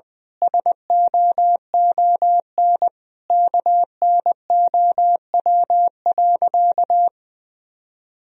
Morse code, heard as a keyed tone, about 20 wpm, 700 Hz. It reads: SOON KNOW.
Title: Sentences from Top 400 Words and Abbreviations 20wpm